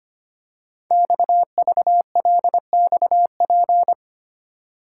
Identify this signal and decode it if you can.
Morse code, heard as a keyed tone, about 25 wpm, 700 Hz. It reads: XVLXP
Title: Code Groups - 5 Letters - 25wpm